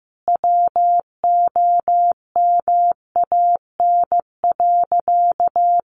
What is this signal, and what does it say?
Morse code, heard as a keyed tone, about 15 wpm, 700 Hz. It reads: WOMAN.
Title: Sentences from Top 1100 Words 15wpm